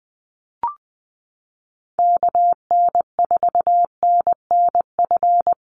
Morse code, heard as a keyed tone, about 19 wpm, 700 Hz. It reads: KN4NNF
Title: USA Call Signs 20wpm